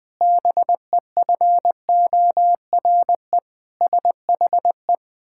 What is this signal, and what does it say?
Morse code, heard as a keyed tone, about 20 wpm, 700 Hz. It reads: BEFORE SHE
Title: Pangram Challenge v2 - 20wpm